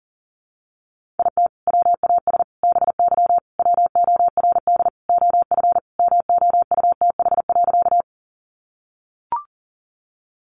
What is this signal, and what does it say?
Morse code, heard as a keyed tone, about 40 wpm, 700 Hz. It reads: IT WAS BY WORD OF MOUTH.
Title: Sentences from Top 1500 Words 40wpm